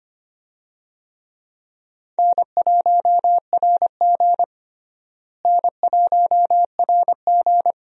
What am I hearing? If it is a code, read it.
Morse code, heard as a keyed tone, about 25 wpm, 700 Hz. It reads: N1RG N1RG
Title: USA Call Signs - 4 Characters - Repeated 2x - 25wpm